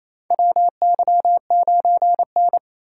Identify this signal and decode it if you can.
Morse code, heard as a keyed tone, about 27 wpm, 700 Hz. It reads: WY9N
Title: USA Call Signs - 8x Character Spacing - 28wpm